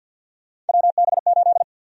Morse code, heard as a keyed tone, about 52 wpm, 700 Hz. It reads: WB8
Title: USA Call Sign Prefixes - 50wpm